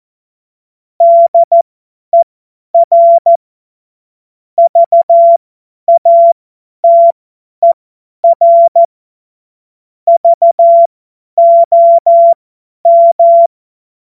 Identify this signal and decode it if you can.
Morse code, heard as a keyed tone, about 14 wpm, spaced as for 10 wpm, 680 Hz. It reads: DER VATER VOM
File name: Morsekurs / Witze / Witz 18. April 2023 50 F70